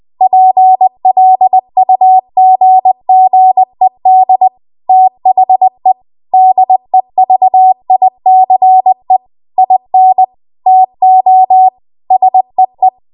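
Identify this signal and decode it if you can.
Morse code, heard as a keyed tone, about 20 wpm, 750 Hz. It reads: PLUGGED THE DEVICE IN TO SEE
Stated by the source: Morse code practice at 20 WPM.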